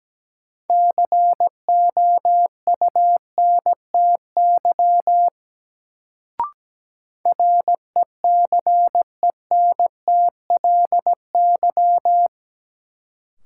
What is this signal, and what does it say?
Morse code, heard as a keyed tone, about 17 wpm, 700 Hz. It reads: COUNTY RECENTLY
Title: Sentences from Top 1200 Words - New Word Drill Alt - 17wpm